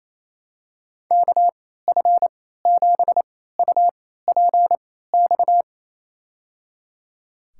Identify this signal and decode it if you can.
Morse code, heard as a keyed tone, about 28 wpm, spaced as for 16 wpm, 700 Hz. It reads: KF7UPX
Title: USA Call Signs - 6 Characters - 3x Character Spacing - 28wpm